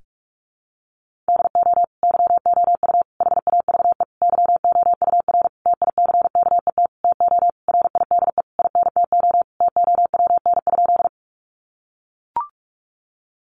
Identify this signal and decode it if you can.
Morse code, heard as a keyed tone, about 44 wpm, 700 Hz. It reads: DO YOU HAVE YOUR TICKET TO RIDE INTO TOWN?
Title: Sentences from Top 1400 Words 45wpm